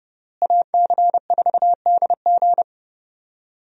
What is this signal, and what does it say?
Morse code, heard as a keyed tone, about 30 wpm, 700 Hz. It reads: AC4DG
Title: USA Call Signs - 5 Characters - Repeated 3x - 30wpm